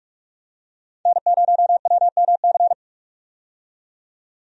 Morse code, heard as a keyed tone, about 46 wpm, 700 Hz. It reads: N0WMC
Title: USA Call Signs - 5 Characters - 45wpm